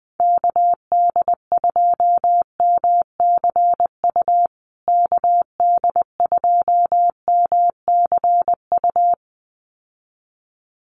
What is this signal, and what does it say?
Morse code, heard as a keyed tone, about 20 wpm, 700 Hz. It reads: KD2MCU KD2MCU
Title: USA Call Signs Repeated 2x 20wpm